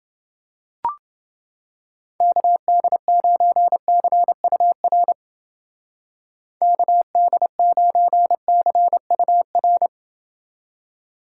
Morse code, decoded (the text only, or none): KD9CUR KD9CUR